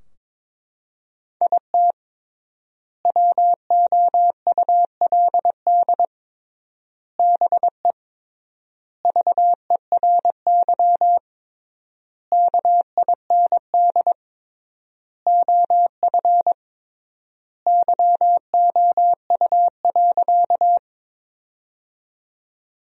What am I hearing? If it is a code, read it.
Morse code, heard as a keyed tone, about 22 wpm, 700 Hz. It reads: IT WOULD BE VERY KIND OF YOU.
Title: 3x Word-Spacing - Sentences from Top 200 Words - 22wpm